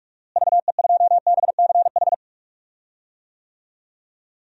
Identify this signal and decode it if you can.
Morse code, heard as a keyed tone, about 46 wpm, 700 Hz. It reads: VE2BCH
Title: DXCC-100 Call Signs - 6 Characters - Repeated 2x - 45wpm